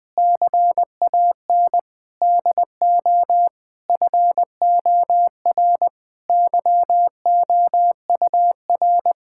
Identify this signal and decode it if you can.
Morse code, heard as a keyed tone, about 20 wpm, 700 Hz. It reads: CAN DO FOR YOUR
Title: Sentences with 9-Word Phrases - 20wpm